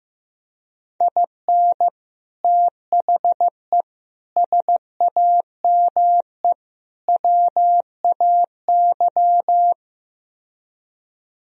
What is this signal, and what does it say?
Morse code, heard as a keyed tone, about 15 wpm, 700 Hz. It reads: IN THE SAME WAY
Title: Speed-Racing - Sets of 4 Words 15wpm